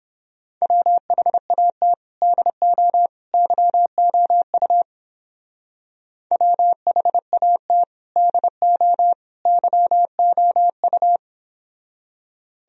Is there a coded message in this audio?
Yes: Morse code from 0.6 to 11.2 s, heard as a keyed tone, about 31 wpm, 700 Hz. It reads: WHAT DO YOU WHAT DO YOU